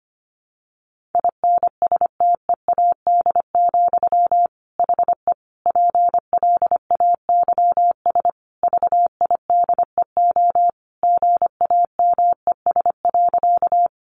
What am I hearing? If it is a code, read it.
Morse code, heard as a keyed tone, about 25 wpm, 700 Hz. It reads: INSTEAD, HE PLAYS VIDEO GAMES.
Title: Additional Easy Phrases 25wpm